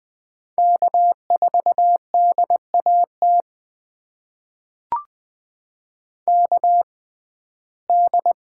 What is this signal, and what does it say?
Morse code, heard as a keyed tone, about 20 wpm, spaced as for 14 wpm, 700 Hz. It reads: K4DAT KD
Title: USA Call Signs - 5 Characters - 6x Character Spacing - 20wpm